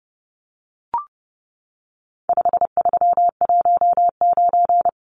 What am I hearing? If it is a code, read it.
Morse code, heard as a keyed tone, about 30 wpm, 700 Hz. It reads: 5319
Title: Numbers - 4-Digit 30wpm